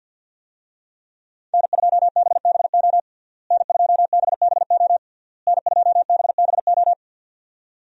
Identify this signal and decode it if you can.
Morse code, heard as a keyed tone, about 53 wpm, 700 Hz. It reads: N2BBO N2BBO N2BBO
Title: USA Call Signs - 5 Characters - Repeated 3x - 50wpm